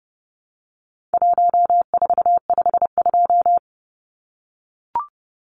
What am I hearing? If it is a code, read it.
Morse code, heard as a keyed tone, about 30 wpm, 700 Hz. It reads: 1452
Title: Numbers - Contest Style - 4-Digit 30wpm